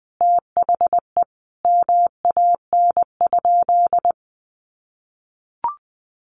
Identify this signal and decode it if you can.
Morse code, heard as a keyed tone, about 20 wpm, 700 Hz. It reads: THE MAN?
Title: Sentences from Top 200 Words 20wpm